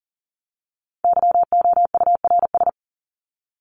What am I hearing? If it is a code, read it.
Morse code, heard as a keyed tone, about 40 wpm, 700 Hz. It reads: YOURS
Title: Top 1751-2000 Words 40wpm